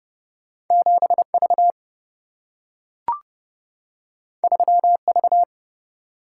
Morse code, heard as a keyed tone, about 31 wpm, 700 Hz. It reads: 7 V 3 V
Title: Single-Letter Single-Number 30wpm